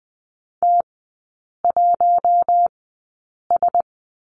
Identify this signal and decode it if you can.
Morse code, heard as a keyed tone, about 20 wpm, spaced as for 8 wpm, 700 Hz. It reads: T1S
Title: 060 Lesson 11e - Letter h - Three Characters 20wpm